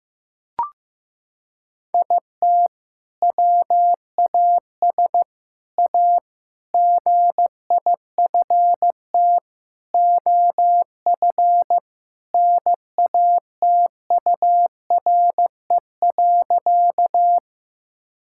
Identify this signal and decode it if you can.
Morse code, heard as a keyed tone, about 15 wpm, 700 Hz. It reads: IT WAS A GIFT OF NATURE.
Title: Sentences from Top 1000 Words - Review Part 6 - 15wpm